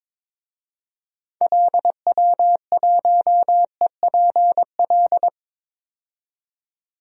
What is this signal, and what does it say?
Morse code, heard as a keyed tone, about 22 wpm, 700 Hz. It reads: LW1EPL
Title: DXCC-100 Call Signs - 6 Characters - 22wpm